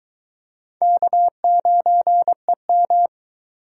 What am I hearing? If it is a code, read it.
Morse code, heard as a keyed tone, about 22 wpm, 700 Hz. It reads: K9EM